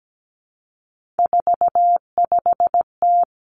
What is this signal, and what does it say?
Morse code, heard as a keyed tone, about 17 wpm, 700 Hz. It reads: 45T
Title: Three Challenging Letter-Number Combinations 17wpm